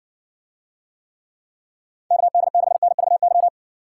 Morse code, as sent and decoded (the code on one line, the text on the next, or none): -.- -.. -.... -. ...- -.--
KD6NVY